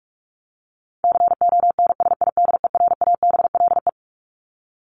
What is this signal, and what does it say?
Morse code, heard as a keyed tone, about 44 wpm, 700 Hz. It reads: CONSIDERABLE
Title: LY Suffix - Top 100 Words with Base Words 45wpm